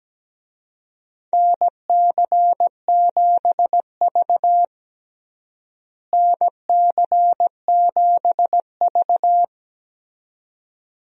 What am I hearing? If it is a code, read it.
Morse code, heard as a keyed tone, about 17 wpm, 700 Hz. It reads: NC7V NC7V
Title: USA Call Signs - 4 Characters - Repeated 2x - 17wpm